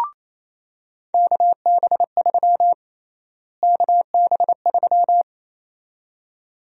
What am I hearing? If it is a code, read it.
Morse code, heard as a keyed tone, about 28 wpm, 700 Hz. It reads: KB3 KB3
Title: USA Call Sign Prefixes - Repeated 2x - 28wpm